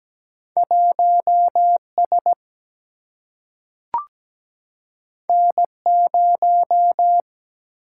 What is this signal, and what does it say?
Morse code, heard as a keyed tone, about 17 wpm, 700 Hz. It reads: 1 S N 0
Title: Single-Letter Single-Number 17wpm